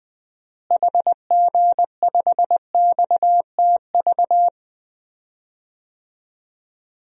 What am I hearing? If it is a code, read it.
Morse code, heard as a keyed tone, about 21 wpm, 700 Hz. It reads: HG5XTV
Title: DXCC-100 Call Signs - 6 Characters - Repeated 2x - 20wpm